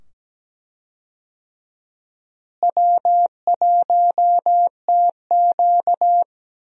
Morse code, heard as a keyed tone, about 17 wpm, 700 Hz. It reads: W1TQ
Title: USA Call Signs - 4 Characters - Repeated 2x - 17wpm